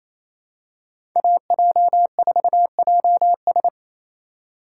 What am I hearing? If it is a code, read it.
Morse code, heard as a keyed tone, about 28 wpm, 700 Hz. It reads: AJ4JS
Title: USA Call Signs - 5 Characters - 2x Character Spacing - 28wpm